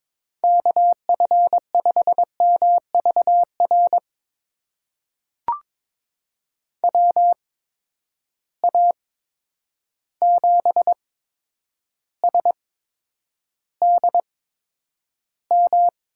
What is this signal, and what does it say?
Morse code, heard as a keyed tone, about 22 wpm, spaced as for 9 wpm, 700 Hz. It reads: KF5MVR WA7SDM
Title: USA Call Signs - 6 Characters - 8x Character Spacing - 22wpm